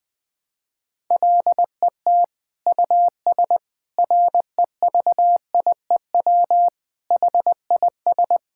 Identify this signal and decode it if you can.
Morse code, heard as a keyed tone, about 21 wpm, 700 Hz. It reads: LET US REVIEW HIS
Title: Sentences from Top 1000 Words - Review Part 6 - 20wpm